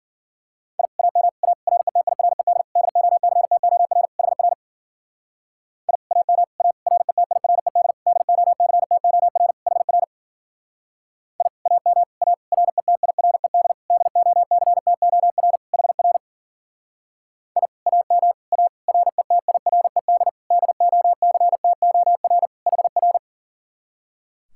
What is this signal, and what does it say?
Morse code, heard as a keyed tone, about 49 wpm, 700 Hz. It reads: I AM A RETIRED DOCTOR HR I AM A RETIRED DOCTOR HR I AM A RETIRED DOCTOR HR I AM A RETIRED DOCTOR HR
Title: Speed-Racing Common QSO Phrases Encore - 40wpm